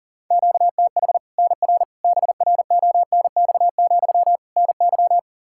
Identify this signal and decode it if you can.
Morse code, heard as a keyed tone, about 37 wpm, 700 Hz. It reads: QTH NR BRONX, NY
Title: Practice QSO - Exchange 1 - Beginning - 40wpm